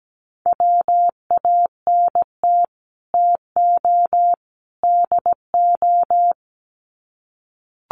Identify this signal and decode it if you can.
Morse code, heard as a keyed tone, about 17 wpm, 700 Hz. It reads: WANT TO DO
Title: Sets of 3 Words 17wpm